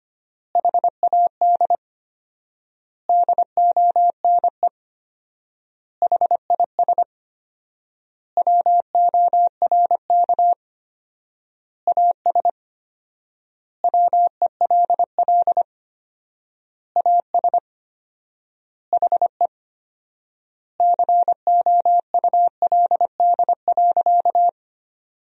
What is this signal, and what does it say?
Morse code, heard as a keyed tone, about 25 wpm, 700 Hz. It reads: HAD DONE HIS WORK AS WELL AS HE COULD.
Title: 4x Word-Spacing - Sentences from Top 500 Words - 25wpm